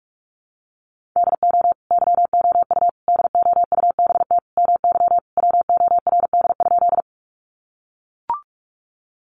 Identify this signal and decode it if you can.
Morse code, heard as a keyed tone, about 44 wpm, 700 Hz. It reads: DO YOU DOUBT MY WORD?
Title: Sentences from Top 1500 Words 45wpm